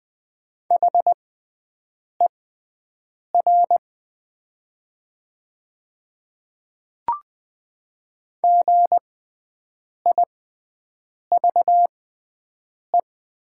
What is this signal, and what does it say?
Morse code, heard as a keyed tone, about 20 wpm, spaced as for 7 wpm, 700 Hz. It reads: HER GIVE
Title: 6x Character Spacing - Top 200 Words - 20wpm